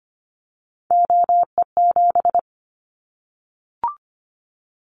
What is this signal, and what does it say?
Morse code, heard as a keyed tone, about 25 wpm, 700 Hz. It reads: OE7
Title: Callsign Prefixes for Top 25 DXCC - 25wpm